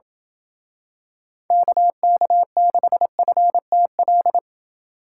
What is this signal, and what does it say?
Morse code, heard as a keyed tone, about 27 wpm, 700 Hz. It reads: KK6FTL